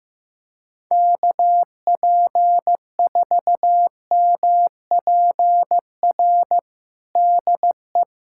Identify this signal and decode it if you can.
Morse code, heard as a keyed tone, about 15 wpm, 700 Hz. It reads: KP4MPR DE
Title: USA Callsign DE Callsign - 6 Characters - 15wpm